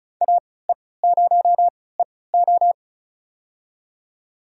Morse code, heard as a keyed tone, about 34 wpm, spaced as for 20 wpm, 700 Hz. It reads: AE0EO
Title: USA Call Signs - 5 Characters - 3x Character Spacing - 35wpm